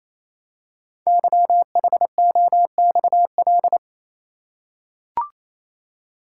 Morse code, heard as a keyed tone, about 28 wpm, 700 Hz. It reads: YHOXL
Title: Code Groups - 5 Letters - 28wpm